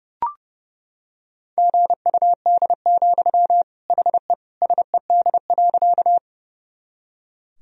Morse code, heard as a keyed tone, about 30 wpm, 700 Hz. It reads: GUD, HE SED.
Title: Sentences from Top 200 Words and Abbreviations 30wpm